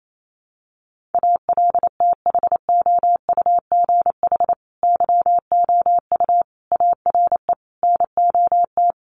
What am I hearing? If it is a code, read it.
Morse code, heard as a keyed tone, about 28 wpm, 700 Hz. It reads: ALTHOUGH YOU ARE NOT
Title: Sentences from Top 1100 Words 28wpm